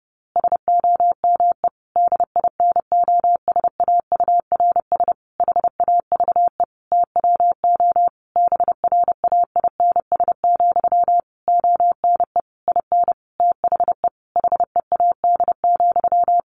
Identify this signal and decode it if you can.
Morse code, heard as a keyed tone, about 30 wpm, 700 Hz. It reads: SOME DINOSAURS HAVE TWO BRAINS, ONE IN THE HEAD,
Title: A Sound Of Thunder 30wpm